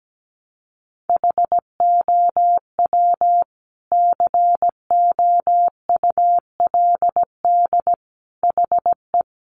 Morse code, heard as a keyed tone, about 17 wpm, 700 Hz. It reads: HOW COULD HE
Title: Sentences from Top 1300 Words 17wpm